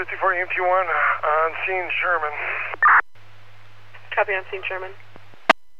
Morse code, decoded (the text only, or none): none